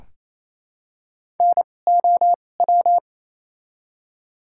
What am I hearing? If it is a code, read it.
Morse code, heard as a keyed tone, about 28 wpm, spaced as for 20 wpm, 700 Hz. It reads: NOW
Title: 2x Character Spacing - Top 300 Words - 28wpm